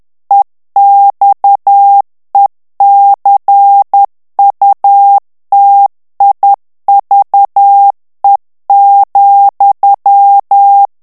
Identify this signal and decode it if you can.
Morse code, heard as a keyed tone, about 10 wpm, 800 Hz. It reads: EXECUTIVE,
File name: ANGLAIS texte clair 3 vitesse 10 mots minute